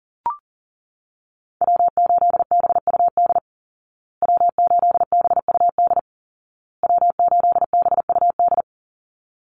Morse code, heard as a keyed tone, about 40 wpm, 700 Hz. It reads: W8BUD W8BUD W8BUD
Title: Call Signs Repeated 2x - 40wpm